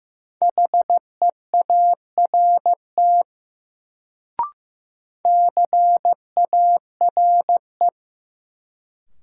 Morse code, heard as a keyed tone, about 15 wpm, 700 Hz. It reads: HEART CARE
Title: Sets of 3 Words - New Word Drill - 15wpm